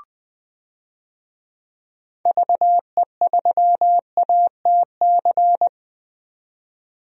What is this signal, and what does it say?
Morse code, heard as a keyed tone, about 20 wpm, 700 Hz. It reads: VE3ATC
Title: DXCC-100 Call Signs - 6 Characters - 20wpm